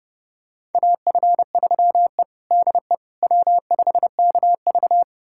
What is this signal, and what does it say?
Morse code, heard as a keyed tone, about 30 wpm, 700 Hz. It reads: AF3E DE W5KV
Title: USA Callsign DE Callsign - 4 Characters - 30wpm